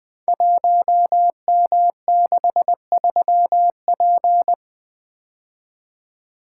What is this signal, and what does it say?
Morse code, heard as a keyed tone, about 20 wpm, 700 Hz. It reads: 1M63P
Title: Code Groups - 5 Letter:Numbers - 20wpm